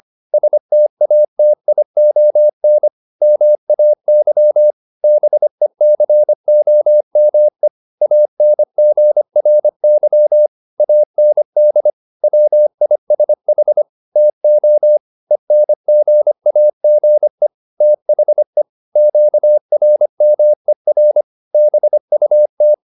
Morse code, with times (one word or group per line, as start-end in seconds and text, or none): 0.3-2.9 STATION
3.2-4.7 MAY
5.0-7.7 BECOME
8.0-10.5 ANGRY
10.8-11.9 AND
12.2-13.8 WISH
14.2-15.0 TO
15.3-17.5 ENGAGE
17.8-18.6 THE
19.0-21.2 QRMER
21.5-22.7 BUT